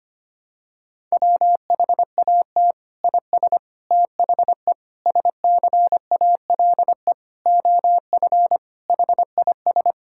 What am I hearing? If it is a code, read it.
Morse code, heard as a keyed tone, about 25 wpm, 700 Hz. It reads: WHAT IS THE SCALE OF HIS
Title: Sentences from Top 1000 Words - Review Part 5 - 25wpm